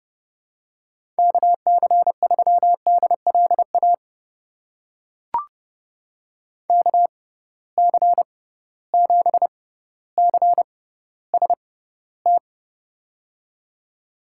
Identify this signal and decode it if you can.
Morse code, heard as a keyed tone, about 31 wpm, spaced as for 15 wpm, 700 Hz. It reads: KC3DLA KC7CST
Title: USA Call Signs - 6 Characters - 6x Character Spacing - 30wpm